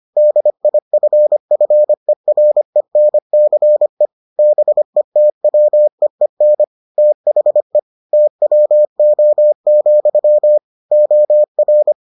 Morse code, heard as a keyed tone, about 25 wpm, 600 Hz. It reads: DIFFERENCE BETWEEN THE TWO, OR